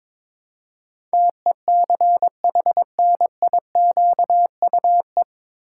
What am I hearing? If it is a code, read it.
Morse code, heard as a keyed tone, about 22 wpm, 700 Hz. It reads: TECHNIQUE
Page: Morse Code Ninja practice 22wpm